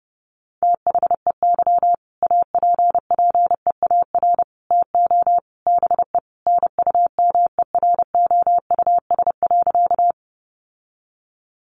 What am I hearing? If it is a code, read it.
Morse code, heard as a keyed tone, about 30 wpm, 700 Hz. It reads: THEY APPEAR TO BE NUMEROUS.
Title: Sentences from Top 1400 Words 30wpm